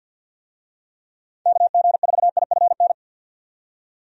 Morse code, heard as a keyed tone, about 51 wpm, 700 Hz. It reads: KG4IFN
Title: USA Call Signs - 6 Characters - 50wpm